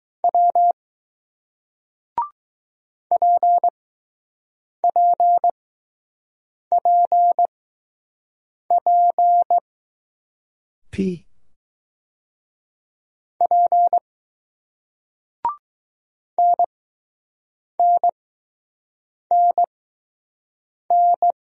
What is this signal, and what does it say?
Morse code, heard as a keyed tone, about 20 wpm, spaced as for 6 wpm, 700 Hz. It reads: W PPPP P NNNN